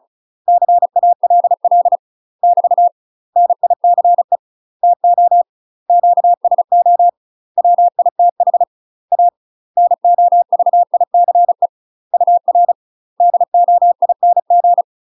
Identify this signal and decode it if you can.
Morse code, heard as a keyed tone, about 36 wpm, 700 Hz. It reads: CALL <BT> NICE TO QSO WITH A NOVICE UR DOING
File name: UK MASTER QSOs TRAINING/35WPM